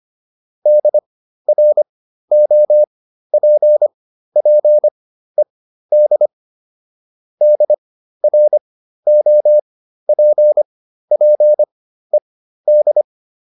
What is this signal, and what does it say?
Morse code, heard as a keyed tone, about 25 wpm, spaced as for 13 wpm, 600 Hz. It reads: DROPPED DROPPED